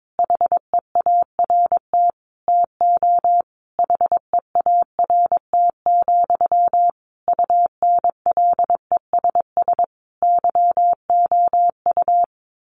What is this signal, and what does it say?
Morse code, heard as a keyed tone, about 22 wpm, 700 Hz. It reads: HEART TO HEART, UNLESS YOU
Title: Sentences from Top 1500 Words 22wpm